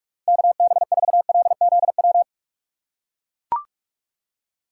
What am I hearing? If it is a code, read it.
Morse code, heard as a keyed tone, about 43 wpm, 700 Hz. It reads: KB4LZW
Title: USA Call Signs - 6 Characters - 4x Character Spacing - 45wpm